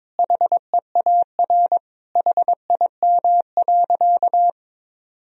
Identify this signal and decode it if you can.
Morse code, heard as a keyed tone, about 22 wpm, 700 Hz. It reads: HEAR HIM.
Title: Sentences from Top 1000 Words - Review Part 1 - 22wpm